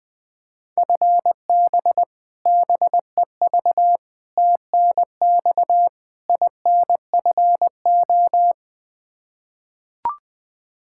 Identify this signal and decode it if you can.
Morse code, heard as a keyed tone, about 21 wpm, 700 Hz. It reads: FB BEV TNX INFO
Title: Practice QSO - Exchange 3 - More Info - 20wpm